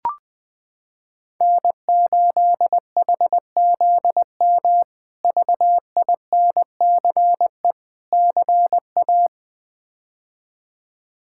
Morse code, heard as a keyed tone, about 20 wpm, 700 Hz. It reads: N8HZM VINCE CA
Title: K1USN SST QSO Practice - 20wpm